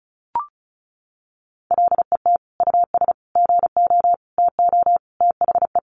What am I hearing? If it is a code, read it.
Morse code, heard as a keyed tone, about 34 wpm, 700 Hz. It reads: LET US GO TO THE
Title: Sentences from Top 1400 Words 35wpm